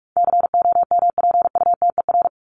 Morse code, heard as a keyed tone, about 44 wpm, 700 Hz. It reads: COMPUTER
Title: Top 501-750 Words 45wpm